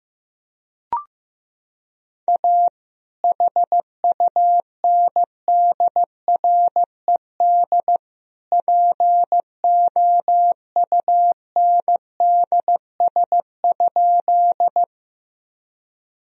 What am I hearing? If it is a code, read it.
Morse code, heard as a keyed tone, about 15 wpm, 700 Hz. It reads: A HUNDRED POUNDS?
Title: Sentences from Top 1000 Words - Review Part 2 - 15wpm